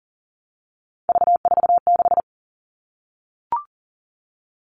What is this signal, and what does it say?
Morse code, heard as a keyed tone, about 40 wpm, 700 Hz. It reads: V46
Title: All Cut Numbers - 3-Digit 40wpm